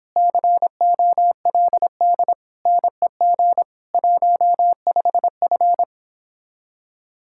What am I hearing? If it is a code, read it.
Morse code, heard as a keyed tone, about 29 wpm, 700 Hz. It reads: COLD NEG 15F